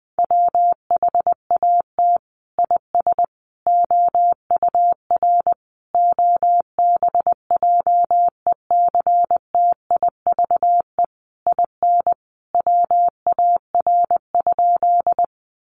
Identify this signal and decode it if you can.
Morse code, heard as a keyed tone, about 20 wpm, 700 Hz. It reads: WHAT IS OUR OBJECTIVE IN WAR?
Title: Sentences from Top 1200 Words 20wpm